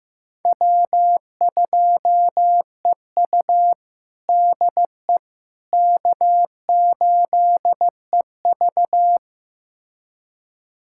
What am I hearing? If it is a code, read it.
Morse code, heard as a keyed tone, about 15 wpm, 700 Hz. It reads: W2EU DE K8EV
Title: USA Callsign DE Callsign - 4 Characters - 15wpm